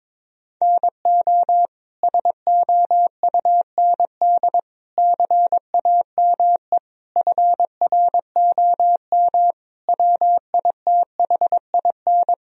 Morse code, heard as a keyed tone, about 22 wpm, 700 Hz. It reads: NO SOUND CAME FROM WITHIN
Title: Sentences from Top 1000 Words - Review Part 4 - 22wpm